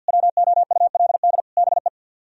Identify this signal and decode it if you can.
Morse code, heard as a keyed tone, about 64 wpm, 700 Hz. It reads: WOULD BE